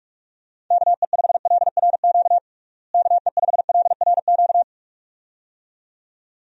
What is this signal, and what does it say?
Morse code, heard as a keyed tone, about 46 wpm, 700 Hz. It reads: KE5LRQ KE5LRQ
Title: USA Call Signs - 6 Characters - Repeated 2x - 45wpm